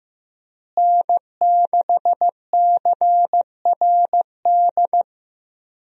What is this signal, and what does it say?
Morse code, heard as a keyed tone, about 15 wpm, 700 Hz. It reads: N6CRD
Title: USA Call Signs - 5 Characters - Repeated 3x - 15wpm